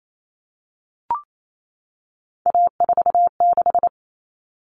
Morse code, heard as a keyed tone, about 28 wpm, 700 Hz. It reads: A46
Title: Common Cut Numbers - 3-Digit 28wpm